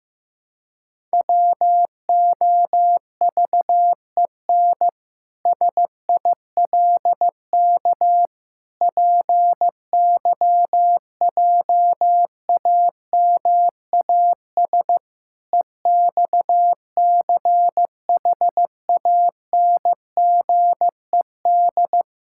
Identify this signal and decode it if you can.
Morse code, heard as a keyed tone, about 15 wpm, 700 Hz. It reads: WOVEN SILK PYJAMAS EXCHANGED
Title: Pangram Challenge v2 - 15wpm